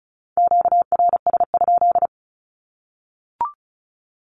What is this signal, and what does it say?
Morse code, heard as a keyed tone, about 34 wpm, 700 Hz. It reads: QRS?
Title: Q-Codes 35wpm